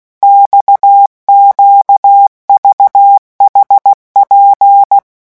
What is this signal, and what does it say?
Morse code, heard as a keyed tone, about 16 wpm, 800 Hz. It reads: XQVHP
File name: morse mp3 / revision5-vitesse 15 mots